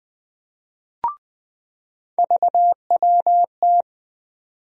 Morse code, heard as a keyed tone, about 20 wpm, 700 Hz. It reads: VWT
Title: USA Call Sign Postfixes - 3 Letters 20wpm